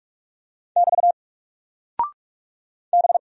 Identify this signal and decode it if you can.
Morse code, heard as a keyed tone, about 46 wpm, 700 Hz. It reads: <BT> B
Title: Single Character (Letter:Number:Punctuation:Prosign) Alt 45wpm